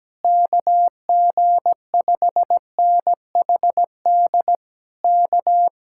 Morse code, heard as a keyed tone, about 17 wpm, 700 Hz. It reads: KG5NHD K
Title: Speed-Racing Common QSO Phrases Encore - 17wpm